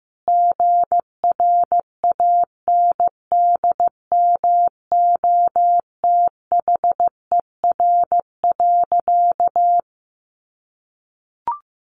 Morse code, heard as a keyed tone, about 15 wpm, 700 Hz. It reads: GRANDMOTHER.
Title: Sentences from Top 1200 Words 15wpm